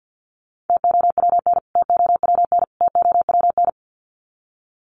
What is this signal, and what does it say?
Morse code, heard as a keyed tone, about 49 wpm, 700 Hz. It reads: TOWN TOWN TOWN
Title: Top 500 Words Repeated 50wpm